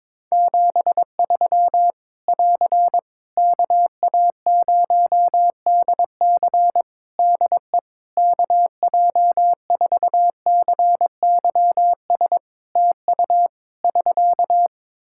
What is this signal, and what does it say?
Morse code, heard as a keyed tone, about 22 wpm, 700 Hz. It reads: 73 <AR> KA0DC DE KJ4CYS TU <SK>
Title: K1USN SST QSO Practice 22wpm